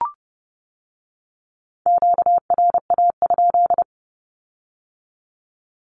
Morse code, heard as a keyed tone, about 30 wpm, 700 Hz. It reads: QRA?
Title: Q-Codes with Spoken Meaning - 30wpm